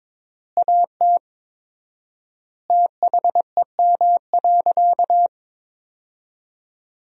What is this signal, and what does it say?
Morse code, heard as a keyed tone, about 22 wpm, 700 Hz. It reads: AT THEM.
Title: 4x Word-Spacing - Sentences from Top 100 Words - 22wpm